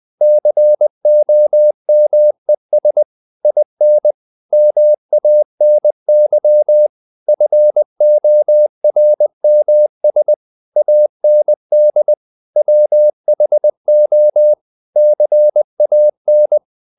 Morse code, heard as a keyed tone, about 20 wpm, 600 Hz. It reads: COMES IN MANY FORMS AND WHO CAN
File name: the old man and the sea part 2 20wpm